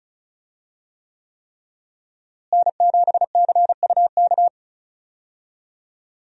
Morse code, heard as a keyed tone, about 36 wpm, 700 Hz. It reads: N7CUK
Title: USA Call Signs - 5 Characters - Repeated 3x - 35wpm